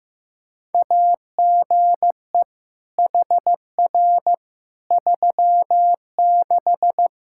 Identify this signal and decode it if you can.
Morse code, heard as a keyed tone, about 15 wpm, 700 Hz. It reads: AGE HR 36
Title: Practice QSO - Exchange 3 - More Info - 15wpm